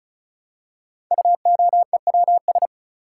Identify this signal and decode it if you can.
Morse code, heard as a keyed tone, about 36 wpm, 700 Hz. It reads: UOEWS
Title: Code Groups - 5 Letters - 35wpm